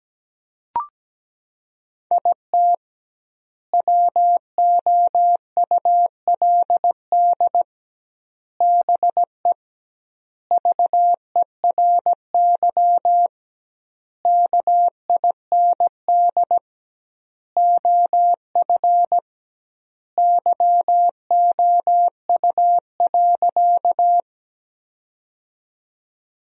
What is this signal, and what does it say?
Morse code, heard as a keyed tone, about 17 wpm, 700 Hz. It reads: IT WOULD BE VERY KIND OF YOU.
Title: 2x Word-Spacing - Sentences from Top 200 Words - 17wpm